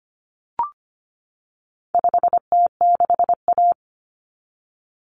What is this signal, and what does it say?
Morse code, heard as a keyed tone, about 25 wpm, 700 Hz. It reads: 5T6A
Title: Common Cut Numbers - 4-Digit 25wpm